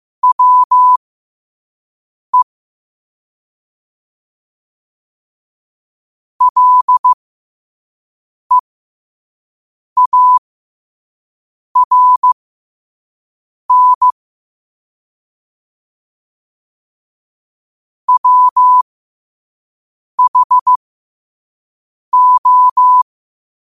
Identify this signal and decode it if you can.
Morse code, heard as a keyed tone, about 15 wpm, spaced as for 5 wpm, 1000 Hz. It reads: WE LEARN WHO
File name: qotdfiles / 5wpm